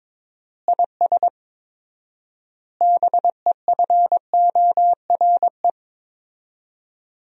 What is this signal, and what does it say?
Morse code, heard as a keyed tone, about 22 wpm, 700 Hz. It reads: IS BEFORE
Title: 4x Word-Spacing - Sentences from Top 200 Words - 22wpm